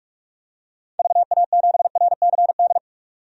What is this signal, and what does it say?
Morse code, heard as a keyed tone, about 42 wpm, 700 Hz. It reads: VA7RCD
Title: DXCC-100 Call Signs - 6 Characters - 45wpm